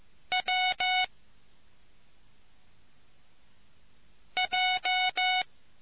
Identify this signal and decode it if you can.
Morse code, heard as a keyed tone, about 15 wpm, 740 Hz. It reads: W J